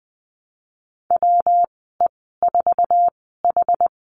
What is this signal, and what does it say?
Morse code, heard as a keyed tone, about 20 wpm, spaced as for 14 wpm, 700 Hz. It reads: WE4H
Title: 140 Lesson 23g - Letter f - Call Signs 20wpm